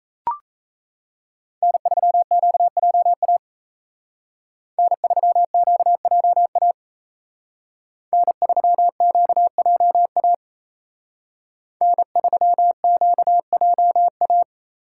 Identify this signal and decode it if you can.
Morse code, heard as a keyed tone, about 36 wpm, 700 Hz. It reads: N3QJA N3QJA N3QJA N3QJA